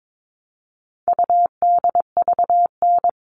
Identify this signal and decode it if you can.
Morse code, heard as a keyed tone, about 22 wpm, 700 Hz. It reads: UDVN
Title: All Cut Numbers - 4-Digit 22wpm